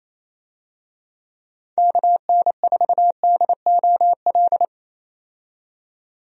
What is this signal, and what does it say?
Morse code, heard as a keyed tone, about 27 wpm, 700 Hz. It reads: KN4DOL
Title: USA Call Signs - 6 Characters - 28wpm